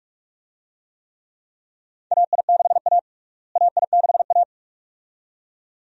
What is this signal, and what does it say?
Morse code, heard as a keyed tone, about 46 wpm, 700 Hz. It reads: AI6A AI6A
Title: USA Call Signs - 4 Characters - Repeated 2x - 45wpm